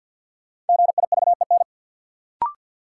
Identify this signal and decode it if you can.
Morse code, heard as a keyed tone, about 51 wpm, 700 Hz. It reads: GIVEN
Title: Sentences from Top 1400 Words - New Word Drill Alt - 50wpm